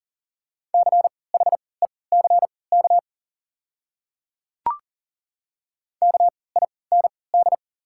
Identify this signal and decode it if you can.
Morse code, heard as a keyed tone, about 40 wpm, spaced as for 23 wpm, 700 Hz. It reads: CHECK KIND
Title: 3x Character Spacing - Top 500 Words - 40wpm